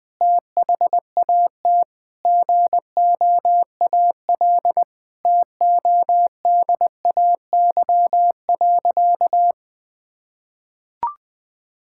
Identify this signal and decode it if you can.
Morse code, heard as a keyed tone, about 20 wpm, 700 Hz. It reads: THAT GOAL TODAY.
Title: Sentences from Top 1000 Words - Review Part 5 - 20wpm